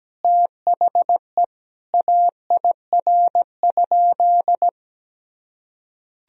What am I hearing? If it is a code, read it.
Morse code, heard as a keyed tone, about 17 wpm, 700 Hz. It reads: THE AIR?
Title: Sentences from Top 1000 Words - Review Part 2 - 17wpm